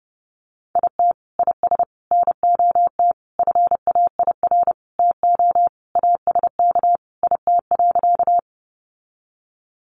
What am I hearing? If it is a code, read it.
Morse code, heard as a keyed tone, about 30 wpm, 700 Hz. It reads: IT IS NOT FAIR TO ASK IT.
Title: Sentences from Top 1300 Words 30wpm